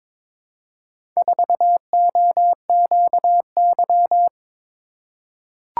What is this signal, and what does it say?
Morse code, heard as a keyed tone, about 22 wpm, 700 Hz. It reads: 4OQY
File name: practice / 22wpm / Four Letter-Number Combinations - 22wpm